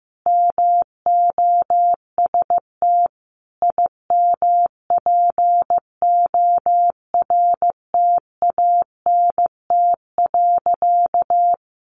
Morse code, heard as a keyed tone, about 14 wpm, 700 Hz. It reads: MOST IMPORTANT.
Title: Sentences from Top 1200 Words 15wpm